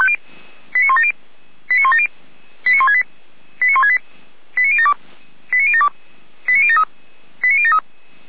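Five-tone selective call (ZVEI), heard as quick runs of five tones, tones of about 70 ms. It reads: 89170 89170 89168 89168 89072 89072 89073 89073
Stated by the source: ZVEI five-tone selective call received on an SDR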